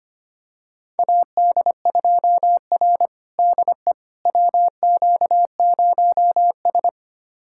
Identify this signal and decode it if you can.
Morse code, heard as a keyed tone, about 25 wpm, 700 Hz. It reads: AD2R DE WQ0S
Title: USA Callsign DE Callsign - 4 Characters - 25wpm